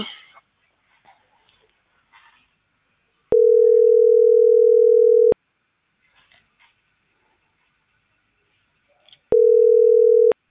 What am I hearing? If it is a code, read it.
Ringback tone (North America), 440 + 480 Hz.